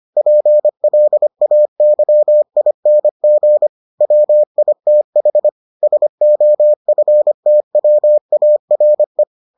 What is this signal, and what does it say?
Morse code, heard as a keyed tone, about 25 wpm, 600 Hz. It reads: PLAYING WITH SOFTWARE